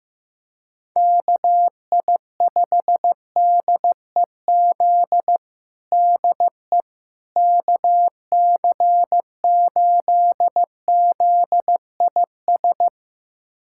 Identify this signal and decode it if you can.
Morse code, heard as a keyed tone, about 15 wpm, 700 Hz. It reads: KI5DEZ DE KC8ZIS
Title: USA Callsign DE Callsign 15wpm